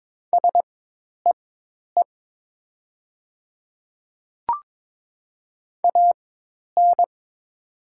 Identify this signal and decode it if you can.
Morse code, heard as a keyed tone, about 22 wpm, spaced as for 10 wpm, 700 Hz. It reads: SEE AN
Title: 4x Character Spacing - Top 100 Words - 22wpm